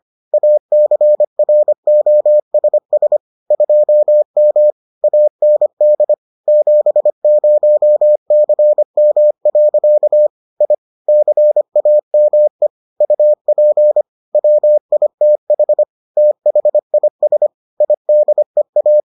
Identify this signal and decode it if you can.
Morse code, heard as a keyed tone, about 25 wpm, 600 Hz. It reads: ACROSS 2M AND 70CM. I CAME UP WITH THIS IDEA